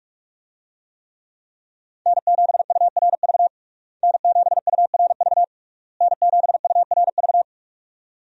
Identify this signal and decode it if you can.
Morse code, heard as a keyed tone, about 42 wpm, 700 Hz. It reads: N7URV N7URV N7URV
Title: USA Call Signs - 5 Characters - Repeated 3x - 45wpm